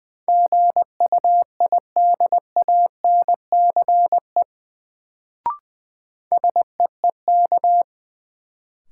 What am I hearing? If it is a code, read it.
Morse code, heard as a keyed tone, about 21 wpm, 700 Hz. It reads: GUIDANCE SEEK
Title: Sentences from Top 1600 Words - New Word Drill Alt - 20wpm